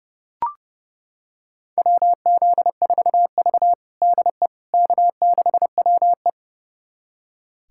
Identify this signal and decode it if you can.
Morse code, heard as a keyed tone, about 32 wpm, 700 Hz. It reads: WZ4V DE K6WE
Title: USA Callsign DE Callsign - 4 Characters - 30wpm